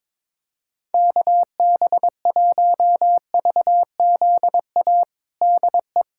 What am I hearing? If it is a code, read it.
Morse code, heard as a keyed tone, about 22 wpm, 700 Hz. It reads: KB1VZA DE
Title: USA Callsign DE Callsign - 6 Characters - 22wpm